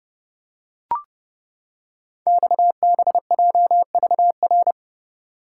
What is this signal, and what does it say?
Morse code, heard as a keyed tone, about 29 wpm, 700 Hz. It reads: XBJVR
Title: Code Groups - 5 Letters - 30wpm